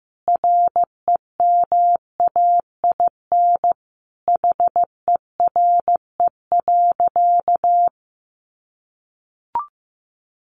Morse code, heard as a keyed tone, about 15 wpm, 700 Hz. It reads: REMAIN HERE.
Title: Sentences from Top 1200 Words 15wpm